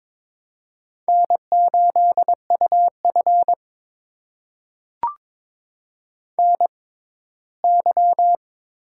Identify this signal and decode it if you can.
Morse code, heard as a keyed tone, about 22 wpm, spaced as for 15 wpm, 700 Hz. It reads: N8UF NY
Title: USA Call Signs - 4 Characters - 6x Character Spacing - 22wpm